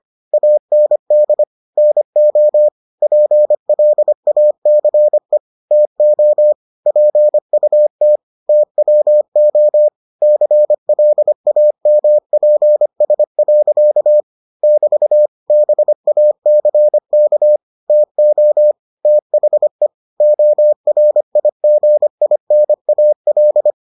Morse code, heard as a keyed tone, about 25 wpm, 600 Hz. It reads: AND NO PLACE TO PUT TWO CLAMPS. <BT> BACK TO THE ORIGINAL